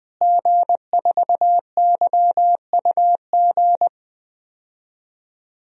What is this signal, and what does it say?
Morse code, heard as a keyed tone, about 20 wpm, 700 Hz. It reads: G4YUG
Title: DXCC-100 Call Signs - 5 Characters - Repeated 3x - 20wpm